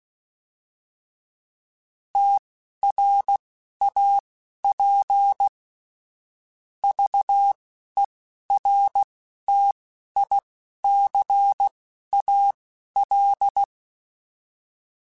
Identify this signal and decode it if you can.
Morse code, heard as a keyed tone, about 16 wpm, spaced as for 11 wpm, 800 Hz. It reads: TRAP VERTICAL
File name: fromSMs / FromSMs mp3s / Antennas@10